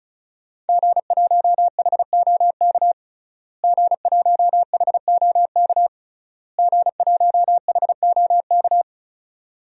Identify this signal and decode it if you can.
Morse code, heard as a keyed tone, about 36 wpm, 700 Hz. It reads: G1HOK G1HOK G1HOK
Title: DXCC-100 Call Signs - 5 Characters - Repeated 3x - 35wpm